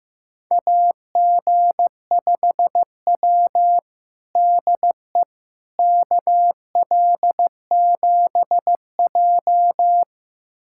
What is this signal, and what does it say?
Morse code, heard as a keyed tone, about 16 wpm, 700 Hz. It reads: AG5W DE KL7J
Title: USA Callsign DE Callsign - 4 Characters - 15wpm